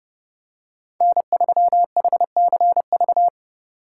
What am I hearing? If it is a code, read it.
Morse code, heard as a keyed tone, about 30 wpm, 700 Hz. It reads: N3HCV
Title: USA Call Signs - 5 Characters - 2x Character Spacing - 30wpm